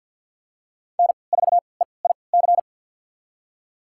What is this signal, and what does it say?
Morse code, heard as a keyed tone, about 52 wpm, spaced as for 29 wpm, 700 Hz. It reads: N4EIC
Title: USA Call Signs - 5 Characters - 3x Character Spacing - 50wpm